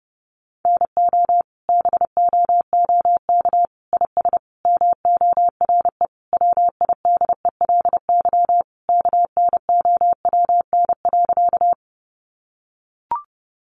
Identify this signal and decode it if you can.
Morse code, heard as a keyed tone, about 30 wpm, 700 Hz. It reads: NO BOOK IS MORE WIDELY KNOWN.
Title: Sentences from Top 1200 Words 30wpm